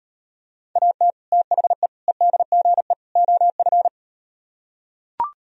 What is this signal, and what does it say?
Morse code, heard as a keyed tone, about 38 wpm, 700 Hz. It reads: AT THE EDGE OF